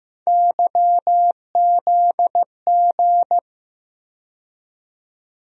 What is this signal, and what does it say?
Morse code, heard as a keyed tone, about 15 wpm, 700 Hz. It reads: YZG
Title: USA Call Sign Postfixes - 3 Letters 15wpm